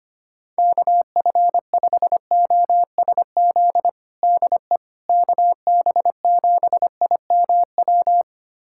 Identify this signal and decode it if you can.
Morse code, heard as a keyed tone, about 25 wpm, 700 Hz. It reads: KF5OSZ DE KB7IMW
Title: USA Callsign DE Callsign 25wpm